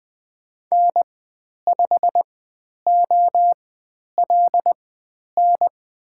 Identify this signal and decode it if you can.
Morse code, heard as a keyed tone, about 20 wpm, spaced as for 10 wpm, 700 Hz. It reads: N5OLN
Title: 095 Lesson 16g - Letter c - Call Signs 20wpm